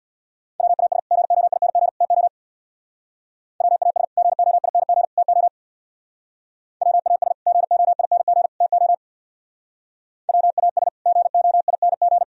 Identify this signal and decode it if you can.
Morse code, heard as a keyed tone, about 63 wpm, 700 Hz. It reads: WAS GOING TO WAS GOING TO WAS GOING TO WAS GOING